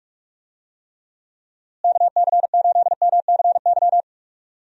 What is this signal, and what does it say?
Morse code, heard as a keyed tone, about 46 wpm, 700 Hz. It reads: KC8MCY
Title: USA Call Signs - 6 Characters - Repeated 3x - 45wpm